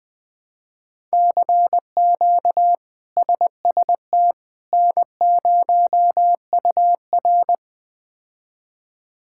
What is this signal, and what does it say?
Morse code, heard as a keyed tone, about 20 wpm, 700 Hz. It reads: CQ SST N0UR
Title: K1USN SST QSO Practice - 20wpm